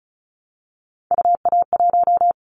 Morse code, heard as a keyed tone, about 34 wpm, 700 Hz. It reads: UA1
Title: Callsign Prefixes for Top 100 DXCC - 35wpm